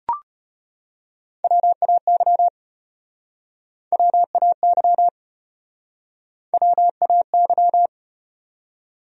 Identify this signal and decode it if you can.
Morse code, heard as a keyed tone, about 34 wpm, 700 Hz. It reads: WAY WAY WAY